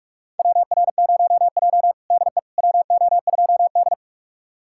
Morse code, heard as a keyed tone, about 47 wpm, 700 Hz. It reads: WR0J DE WO2D
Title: USA Callsign DE Callsign - 4 Characters - 45wpm